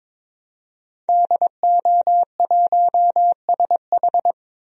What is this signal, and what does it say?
Morse code, heard as a keyed tone, about 22 wpm, 700 Hz. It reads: DO1SH
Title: DXCC-100 Call Signs - 5 Characters - 22wpm